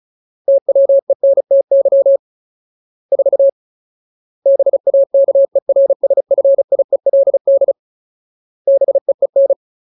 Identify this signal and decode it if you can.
Morse code, heard as a keyed tone, about 34 wpm, 550 Hz. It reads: TWENTY 4 BAKERSFIELD BEEN
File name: Ultimate Word List / 35WPM Ultimate